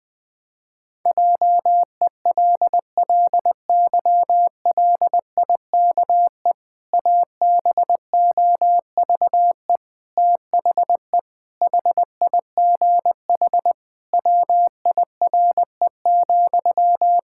Morse code, heard as a keyed tone, about 21 wpm, 700 Hz. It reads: JELLYLIKE ABOVE THE HIGH WIRE,
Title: Pangram Challenge v2 - 20wpm